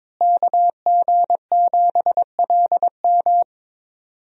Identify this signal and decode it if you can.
Morse code, heard as a keyed tone, about 21 wpm, 700 Hz. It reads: KG7LM
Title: USA Call Signs - 5 Characters - Repeated 2x - 22wpm